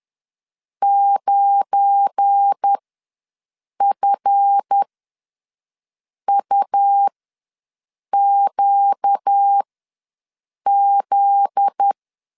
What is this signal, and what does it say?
Morse code, heard as a keyed tone, about 10 wpm, spaced as for 6 wpm, 800 Hz. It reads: 9FUQZ
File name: LEZIONI / 9 2867 ws 10wpm 4215